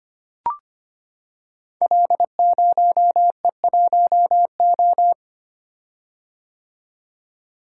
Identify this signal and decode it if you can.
Morse code, heard as a keyed tone, about 25 wpm, 700 Hz. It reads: L0E1O
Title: Code Groups - 5 Letter:Numbers - 25wpm